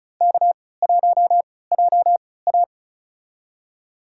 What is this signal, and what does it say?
Morse code, heard as a keyed tone, about 34 wpm, spaced as for 20 wpm, 700 Hz. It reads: K1JA
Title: USA Call Signs - 4 Characters - 3x Character Spacing - 35wpm